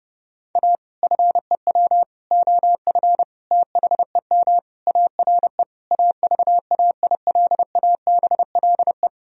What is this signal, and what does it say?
Morse code, heard as a keyed tone, about 28 wpm, 700 Hz. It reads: A FEW OF THEM ARE AVAILABLE
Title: Sentences from Top 1000 Words - Review Part 3 - 30wpm